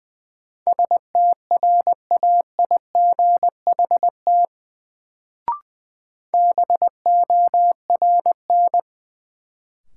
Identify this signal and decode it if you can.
Morse code, heard as a keyed tone, about 20 wpm, 700 Hz. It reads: STRAIGHT BORN
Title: Sentences from Top 1000 Words - New Word Drill - 20wpm